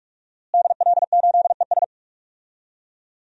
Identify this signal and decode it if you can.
Morse code, heard as a keyed tone, about 46 wpm, 700 Hz. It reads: DL8ES
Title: DXCC-100 Call Signs - 5 Characters - Repeated 3x - 45wpm